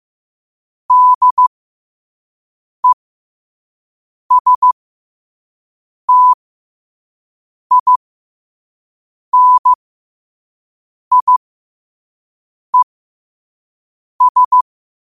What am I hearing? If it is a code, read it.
Morse code, heard as a keyed tone, about 15 wpm, spaced as for 5 wpm, 1000 Hz. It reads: DESTINIES
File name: qotdfiles / 5wpm